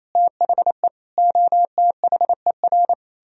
Morse code, heard as a keyed tone, about 37 wpm, 700 Hz. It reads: THE OTHER